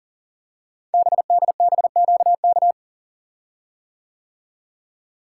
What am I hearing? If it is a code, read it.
Morse code, heard as a keyed tone, about 38 wpm, 700 Hz. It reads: BDBQK
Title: Code Groups - 5 Letters - 40wpm